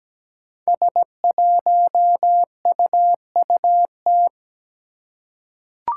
Morse code, heard as a keyed tone, about 17 wpm, 700 Hz. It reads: S1UUT
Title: Five Challenging Letter-Number Combinations 17wpm